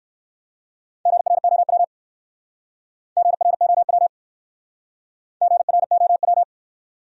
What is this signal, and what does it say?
Morse code, heard as a keyed tone, about 62 wpm, 690 Hz. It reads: GROW GROW GROW